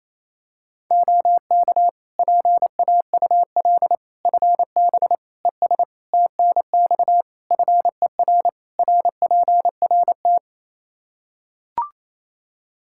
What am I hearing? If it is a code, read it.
Morse code, heard as a keyed tone, about 27 wpm, 700 Hz. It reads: OK PAUL FB ES TNX FER RPRT
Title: Practice QSO - Exchange 2 - Info - 28wpm